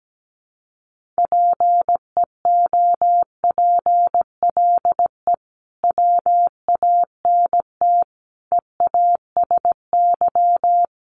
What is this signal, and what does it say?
Morse code, heard as a keyed tone, about 17 wpm, 700 Hz. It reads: PEOPLE WANT EASY
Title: Sentences from Top 1500 Words 17wpm